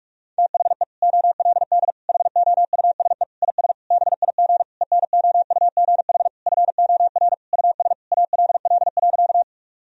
Morse code, heard as a keyed tone, about 42 wpm, 700 Hz. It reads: THE OLD HOUSE IS BIG ENOUGH FOR US ALL.
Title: Sentences from Top 1000 Words - Review Part 1 - 45wpm